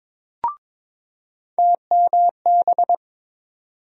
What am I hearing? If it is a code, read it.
Morse code, heard as a keyed tone, about 21 wpm, 700 Hz. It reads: TMB
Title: USA Call Sign Postfixes - 3 Letters 22wpm